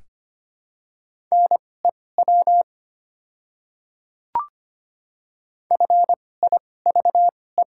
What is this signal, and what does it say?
Morse code, heard as a keyed tone, about 25 wpm, spaced as for 18 wpm, 700 Hz. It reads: NEW FIVE
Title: 2x Character Spacing - Top 500 Words - 25wpm